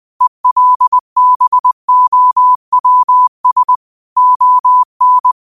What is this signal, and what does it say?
Morse code, heard as a keyed tone, about 20 wpm, 1000 Hz. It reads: ELBOWS ON
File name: qotdfiles / 20wpm